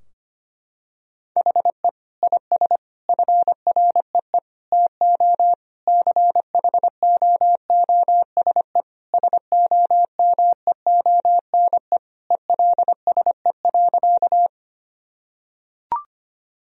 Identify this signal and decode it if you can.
Morse code, heard as a keyed tone, about 25 wpm, 700 Hz. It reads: HE IS FREE TO CHOOSE SOMEONE ELSE.
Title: Sentences from Top 1000 Words - Review Part 4 - 25wpm